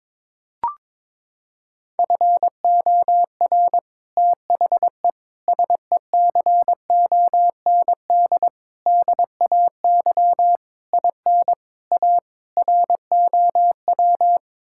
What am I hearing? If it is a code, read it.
Morse code, heard as a keyed tone, about 22 wpm, 700 Hz. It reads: FOR THE SECOND DAY IN A ROW
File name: practice / 22wpm / Sets of 7 Words - 22wpm